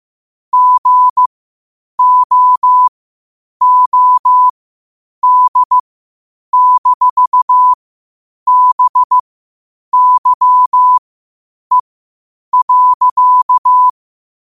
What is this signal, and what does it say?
Morse code, heard as a keyed tone, about 15 wpm, spaced as for 8 wpm, 1000 Hz. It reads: GOOD-BYE.
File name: qotdfiles / 8wpm